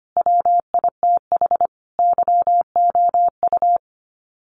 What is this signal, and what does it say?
Morse code, heard as a keyed tone, about 25 wpm, 700 Hz. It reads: WITH YOU
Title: Sets of 5 Words Encore 25wpm